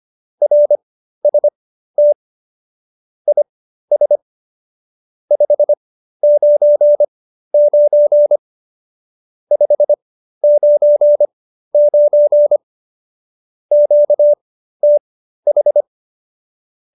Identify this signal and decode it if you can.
Morse code, heard as a keyed tone, about 25 wpm, spaced as for 13 wpm, 590 Hz. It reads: RST IS 599 599 QTH